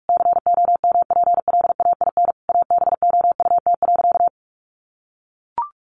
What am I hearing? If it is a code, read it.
Morse code, heard as a keyed tone, about 44 wpm, 700 Hz. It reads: COMPLAIN ABOUT.
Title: Sentences from Top 1300 Words 45wpm